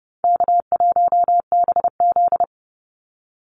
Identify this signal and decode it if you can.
Morse code, heard as a keyed tone, about 30 wpm, 700 Hz. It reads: K1BZ
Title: Easy Call Signs - 30wpm